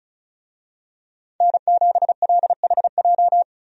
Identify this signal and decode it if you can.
Morse code, heard as a keyed tone, about 36 wpm, 700 Hz. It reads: N7LHJ
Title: USA Call Signs - 5 Characters - 35wpm